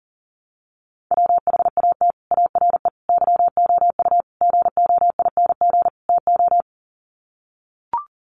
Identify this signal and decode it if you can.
Morse code, heard as a keyed tone, about 37 wpm, 700 Hz. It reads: WHAT ARE YOU GOING TO
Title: Sets of 5 Words 40wpm